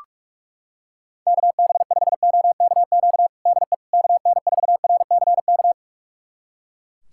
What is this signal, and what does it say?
Morse code, heard as a keyed tone, about 41 wpm, 700 Hz. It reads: KB5OKQ DE KN4RCK
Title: USA Callsign DE Callsign 45wpm